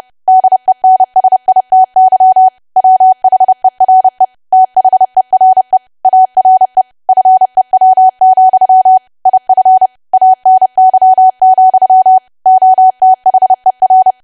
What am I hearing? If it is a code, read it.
Morse code, heard as a keyed tone, about 30 wpm, 750 Hz. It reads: DENSITY WHERE THERE ARE FEW, IF ANY, OTHER
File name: cwa / W1AW-Level3 / 30WPM